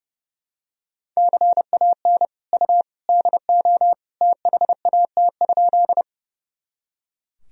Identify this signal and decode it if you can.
Morse code, heard as a keyed tone, about 31 wpm, 700 Hz. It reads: CAN U DO THAT?
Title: Sentences from Top 100 Words and Abbreviations 30wpm